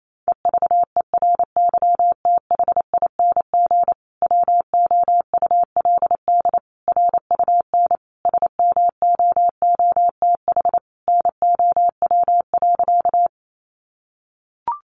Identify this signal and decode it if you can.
Morse code, heard as a keyed tone, about 28 wpm, 700 Hz. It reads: EVERYTHING WOULD RUN SMOOTH NOW.
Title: Sentences from Top 1400 Words 28wpm